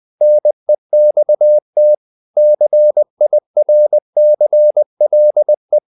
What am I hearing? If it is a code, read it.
Morse code, heard as a keyed tone, about 20 wpm, 600 Hz. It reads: NEXT CIRCLE
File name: the old man and the sea part 2 20wpm